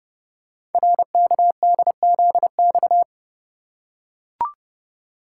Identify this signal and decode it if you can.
Morse code, heard as a keyed tone, about 30 wpm, 700 Hz. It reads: RKDZX
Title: Code Groups - 5 Letters - 30wpm